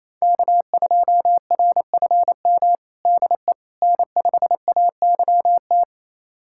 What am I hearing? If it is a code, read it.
Morse code, heard as a keyed tone, about 27 wpm, 700 Hz. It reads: K2RFM DE N5AYT
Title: USA Callsign DE Callsign - 5 Characters - 28wpm